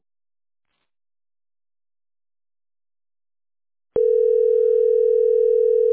Ringback tone (North America), 440 + 480 Hz.